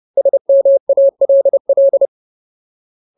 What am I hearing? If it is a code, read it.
Morse code, heard as a keyed tone, about 30 wpm, 550 Hz. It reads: SMALL
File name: Ultimate Word List / 30WPM Ultimate